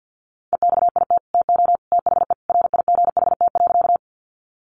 Morse code, heard as a keyed tone, about 50 wpm, 700 Hz. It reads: EXIT TO THE RIGHT.
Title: Sentences from Top 1100 Words 50wpm